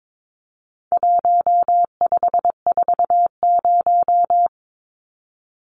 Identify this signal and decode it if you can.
Morse code, heard as a keyed tone, about 22 wpm, 700 Hz. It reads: 1540
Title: Numbers - 4-Digit 22wpm